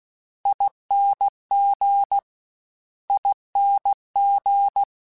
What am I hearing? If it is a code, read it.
Morse code, heard as a keyed tone, about 16 wpm, 800 Hz. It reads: ING ING
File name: General / General-mp3s / General @15